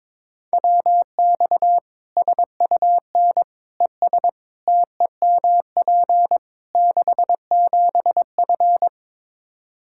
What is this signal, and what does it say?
Morse code, heard as a keyed tone, about 22 wpm, 700 Hz. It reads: WX SUN ES TEMP 67F
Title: Practice QSO - Exchange 2 - Info - 22wpm